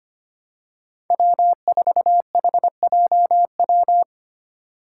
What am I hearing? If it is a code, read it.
Morse code, heard as a keyed tone, about 25 wpm, 700 Hz. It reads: W4HJW
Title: Five Challenging Letter-Number Combinations 25wpm